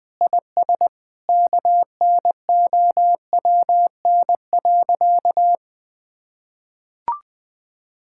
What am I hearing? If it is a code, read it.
Morse code, heard as a keyed tone, about 20 wpm, 700 Hz. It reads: IS KNOWN.
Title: Sentences with 3-Word Phrases - 20wpm